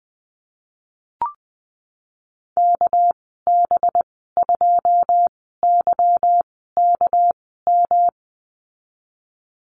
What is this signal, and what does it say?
Morse code, heard as a keyed tone, about 20 wpm, spaced as for 14 wpm, 700 Hz. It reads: KB2YKM
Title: 245 Lesson 39g - Letter z - Call Signs 20wpm